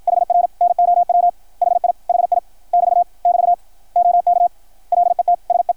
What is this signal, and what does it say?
Morse code, heard as a keyed tone, about 53 wpm, 700 Hz. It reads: HR NOW HI HI <BT> <BT> OK LET SE
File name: archived / CRSnet